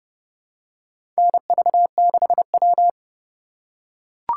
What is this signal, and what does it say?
Morse code, heard as a keyed tone, about 30 wpm, 700 Hz. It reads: NV6W
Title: USA Call Signs - 4 Characters - 3x Character Spacing - 30wpm